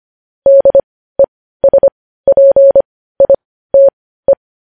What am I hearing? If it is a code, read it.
Morse code, heard as a keyed tone, about 25 wpm, spaced as for 15 wpm, 550 Hz. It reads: DESPITE
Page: News headlines in morse code at 15 wpm